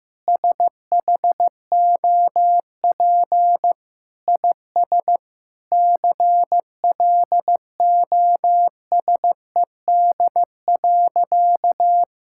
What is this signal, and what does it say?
Morse code, heard as a keyed tone, about 14 wpm, 700 Hz. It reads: SHOP IS CLOSED.
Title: Sentences from Top 1600 Words 15wpm